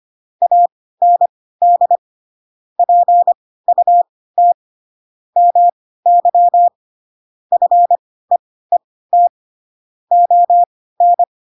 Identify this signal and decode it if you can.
Morse code, heard as a keyed tone, about 25 wpm, spaced as for 16 wpm, 700 Hz. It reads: AND PUT MY FEET ON
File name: Pinocchio / Pinocchio ch7 10 25 ik1hge